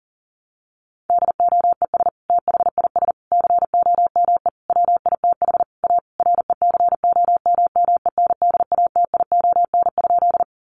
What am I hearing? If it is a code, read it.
Morse code, heard as a keyed tone, about 40 wpm, 700 Hz. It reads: DOES THIS COME WITH A RECOMMENDATION?
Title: Sentences from Top 1400 Words 40wpm